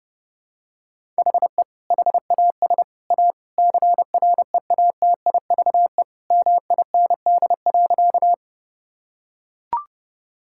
Morse code, heard as a keyed tone, about 30 wpm, 700 Hz. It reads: HE HAS A CREATIVE MIND.
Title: Sentences from Top 1000 Words - Review Part 5 - 30wpm